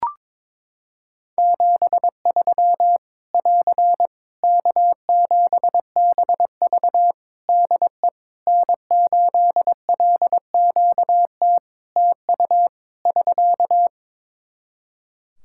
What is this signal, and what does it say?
Morse code, heard as a keyed tone, about 22 wpm, 700 Hz. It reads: 73 <AR> K7BV DE N8LQT TU <SK>
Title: Practice QSO - Exchange 4 - Ending - 22wpm